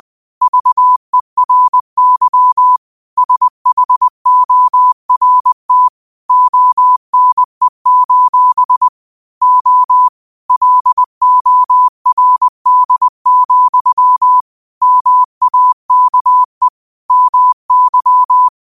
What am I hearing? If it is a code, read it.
Morse code, heard as a keyed tone, about 20 wpm, 1000 Hz. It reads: VERY SHORT ONE: O LORD, MAKE MY